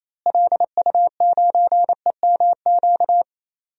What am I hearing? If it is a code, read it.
Morse code, heard as a keyed tone, about 28 wpm, 700 Hz. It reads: LU9EMQ
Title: DXCC-100 Call Signs - 6 Characters - Repeated 2x - 28wpm